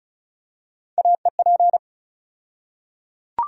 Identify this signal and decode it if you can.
Morse code, heard as a keyed tone, about 34 wpm, 700 Hz. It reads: AEP
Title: USA Call Sign Postfixes - 3 Letters 35wpm